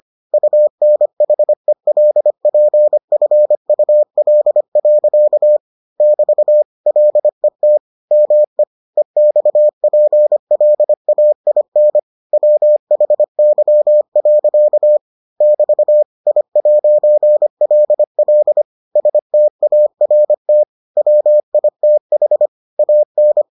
Morse code, heard as a keyed tone, about 25 wpm, 600 Hz. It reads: UNHELPFUL. <BT> LET ME EXPLAIN WHY. <BT> I'LL START WITH AN